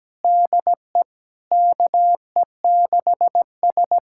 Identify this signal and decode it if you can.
Morse code, heard as a keyed tone, about 17 wpm, 700 Hz. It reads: DE KE6S
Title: USA Callsign DE Callsign - 4 Characters - 17wpm